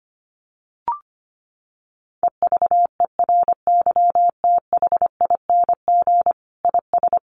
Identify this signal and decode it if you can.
Morse code, heard as a keyed tone, about 25 wpm, 700 Hz. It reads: EVERYTHING IS
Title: Sentences from Top 700 Words 25wpm